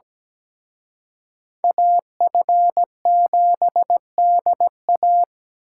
Morse code, heard as a keyed tone, about 20 wpm, 700 Hz. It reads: AF7DA
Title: Speed Racing - USA Call Signs 17wpm